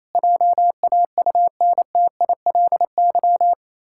Morse code, heard as a keyed tone, about 28 wpm, 700 Hz. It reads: JAUNTILY
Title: Pangram Challenge v2 - 28wpm